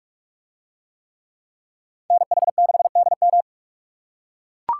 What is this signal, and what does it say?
Morse code, heard as a keyed tone, about 44 wpm, 700 Hz. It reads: NH6DM